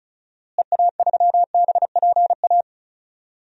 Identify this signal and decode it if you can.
Morse code, heard as a keyed tone, about 34 wpm, 700 Hz. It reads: EA3BPA
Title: DXCC-100 Call Signs - 6 Characters - Repeated 2x - 35wpm